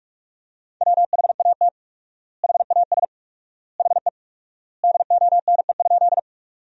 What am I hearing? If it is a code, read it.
Morse code, heard as a keyed tone, about 46 wpm, 690 Hz. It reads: WHAT HAS HE DONE?
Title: 4x Word-Spacing - Sentences from Top 500 Words - 45wpm